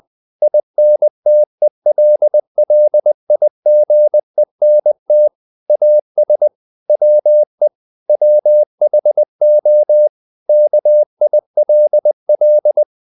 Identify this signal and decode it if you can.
Morse code, heard as a keyed tone, about 20 wpm, 600 Hz. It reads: INTELLIGENT AS WE WHO KILL
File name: the old man and the sea / the old man and the sea part 1 20wpm